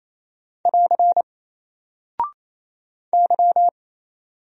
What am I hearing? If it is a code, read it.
Morse code, heard as a keyed tone, about 27 wpm, 700 Hz. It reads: <AR> Y
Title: Single Character (Letter:Number:Punctuation:Prosign) 28wpm